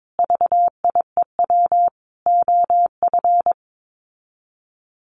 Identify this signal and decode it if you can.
Morse code, heard as a keyed tone, about 22 wpm, 700 Hz. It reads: VIEW OF
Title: Sets of 5 Words 22wpm